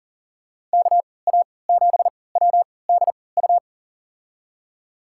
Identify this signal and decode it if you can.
Morse code, heard as a keyed tone, about 37 wpm, spaced as for 22 wpm, 700 Hz. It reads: KA7WDU
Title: USA Call Signs - 6 Characters - 3x Character Spacing - 40wpm